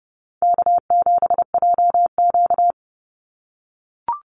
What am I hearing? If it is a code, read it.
Morse code, heard as a keyed tone, about 30 wpm, 700 Hz. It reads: K7JQ
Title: Easy Call Signs - 30wpm